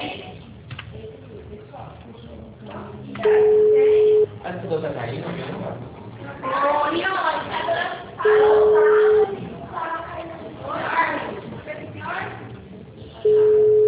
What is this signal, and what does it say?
Ringback tone (Europe, 425 Hz), with speech.